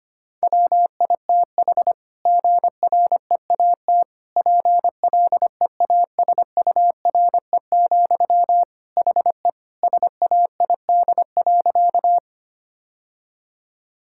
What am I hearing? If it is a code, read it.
Morse code, heard as a keyed tone, about 25 wpm, 700 Hz. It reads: WITH GREAT PLEASURE, HE SAID.
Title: Sentences from Top 1600 Words 25wpm